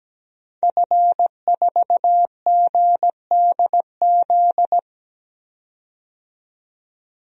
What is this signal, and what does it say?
Morse code, heard as a keyed tone, about 17 wpm, 700 Hz. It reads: F4GDZ
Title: DXCC-100 Call Signs - 5 Characters - 17wpm